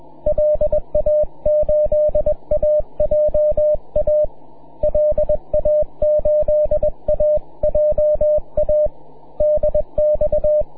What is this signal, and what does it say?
Morse code, heard as a keyed tone, about 20 wpm, 600 Hz. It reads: LA8AJA LA8AJA DX
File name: wp / k4mm 6m cw